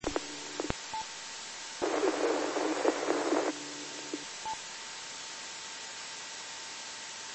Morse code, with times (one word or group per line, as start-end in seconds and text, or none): none